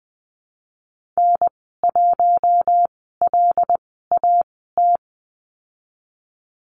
Morse code, heard as a keyed tone, about 20 wpm, spaced as for 15 wpm, 700 Hz. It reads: N1LAT
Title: 082 Lesson 14g - Number 2 - Call Signs 20wpm